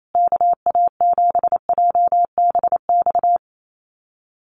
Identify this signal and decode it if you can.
Morse code, heard as a keyed tone, about 28 wpm, 700 Hz. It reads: KA7JBX
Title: USA Call Signs Repeated 3x 28wpm